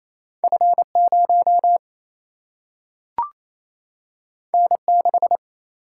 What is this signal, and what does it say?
Morse code, heard as a keyed tone, about 28 wpm, 700 Hz. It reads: F 0 N 6
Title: Single-Letter Single-Number 28wpm